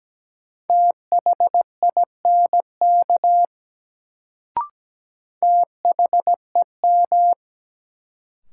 Morse code, heard as a keyed tone, about 17 wpm, 700 Hz. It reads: THINK THEM
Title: Sets of 2 Words - New Word Drill - 17wpm